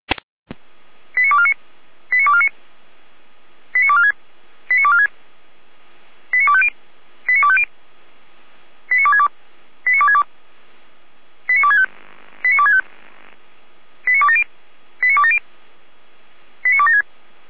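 Five-tone selective call (ZVEI), heard as quick runs of five tones, tones of about 70 ms. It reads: 89249 89249 89246 89246 89250 89250 89262 89262 89265 89265 89270 89270 89276
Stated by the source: ZVEI five-tone selective call received by SDR, code 89230, archived by Feuerwehr Orlamünde.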